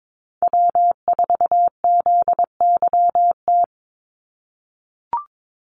Morse code, heard as a keyed tone, about 22 wpm, 700 Hz. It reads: W4ZYT
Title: Call Signs - 22wpm